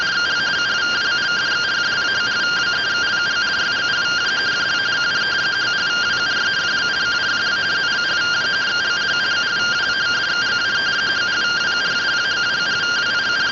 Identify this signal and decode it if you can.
Two-tone teleprinter signal, about 170 Hz shift.